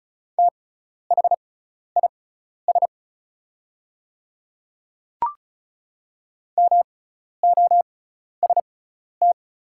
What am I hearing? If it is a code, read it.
Morse code, heard as a keyed tone, about 36 wpm, spaced as for 12 wpm, 700 Hz. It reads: THIS MOST
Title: 6x Character Spacing - Top 100 Words - 35wpm